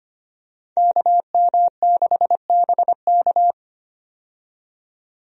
Morse code, heard as a keyed tone, about 25 wpm, 700 Hz. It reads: KM6BK
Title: USA Call Signs - 5 Characters - 25wpm